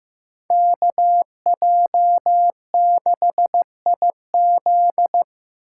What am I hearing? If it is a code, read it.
Morse code, heard as a keyed tone, about 15 wpm, 700 Hz. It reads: KJ6IZ
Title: USA Call Signs - 5 Characters - 2x Character Spacing - 15wpm